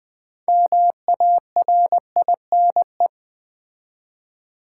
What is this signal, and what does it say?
Morse code, heard as a keyed tone, about 20 wpm, 700 Hz. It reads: MARINE
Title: 6 Letter Words - First 3 Letters Given - 20wpm